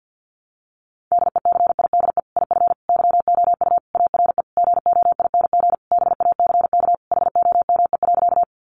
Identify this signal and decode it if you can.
Morse code, heard as a keyed tone, about 50 wpm, 700 Hz. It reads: DECIDE IF YOU ARE GOING BACK HOME.
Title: Sentences from Top 500 Words 50wpm